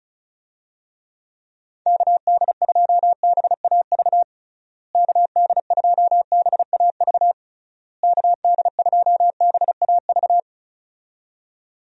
Morse code, heard as a keyed tone, about 36 wpm, 700 Hz. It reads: KD2BAV KD2BAV KD2BAV
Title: USA Call Signs - 6 Characters - Repeated 3x - 35wpm